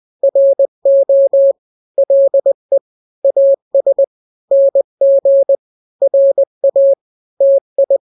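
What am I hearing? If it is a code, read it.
Morse code, heard as a keyed tone, about 20 wpm, 550 Hz. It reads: RO LE AS NG RA TI
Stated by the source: Morse drill of two-letter n-grams at 20/18 WPM (Farnsworth), from KK7RR.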